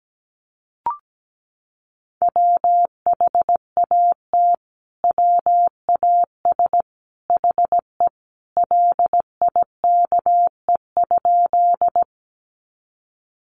Morse code, heard as a keyed tone, about 17 wpm, 700 Hz. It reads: WHAT WAS HE LIKE?
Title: Sentences from Top 100 Words 17wpm